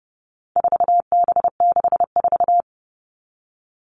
Morse code, heard as a keyed tone, about 30 wpm, 700 Hz. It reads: 4B64
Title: All Cut Numbers - 4-Digit 30wpm